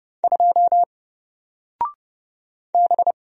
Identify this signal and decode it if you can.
Morse code, heard as a keyed tone, about 31 wpm, 700 Hz. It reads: 2 B
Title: Single Letter-Number Alt 30wpm